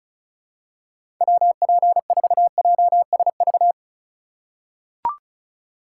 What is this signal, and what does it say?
Morse code, heard as a keyed tone, about 34 wpm, 700 Hz. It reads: WP4JSV
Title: USA Call Signs - 6 Characters - 3x Character Spacing - 35wpm